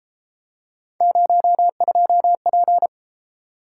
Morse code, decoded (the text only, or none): 02P